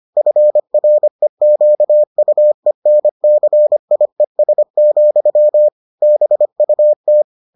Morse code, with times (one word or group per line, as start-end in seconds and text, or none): 0.2-5.7 FREQUENCIES,
6.0-7.2 BUT